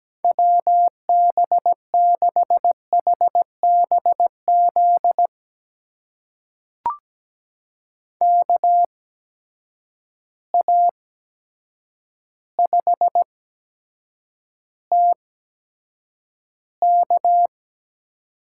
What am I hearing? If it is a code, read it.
Morse code, heard as a keyed tone, about 17 wpm, spaced as for 8 wpm, 700 Hz. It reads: WB6HBZ KA5TK
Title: USA Call Signs - 6 Characters - 8x Character Spacing - 17wpm